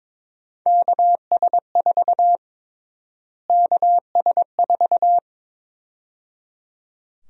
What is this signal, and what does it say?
Morse code, heard as a keyed tone, about 22 wpm, 700 Hz. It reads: KS4 KS4
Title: USA Call Sign Prefixes - Repeated 2x - 22wpm